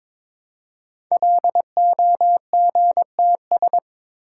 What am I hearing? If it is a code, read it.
Morse code, heard as a keyed tone, about 22 wpm, 700 Hz. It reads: LOGTS
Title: Code Groups - 5 Letter:Numbers - 22wpm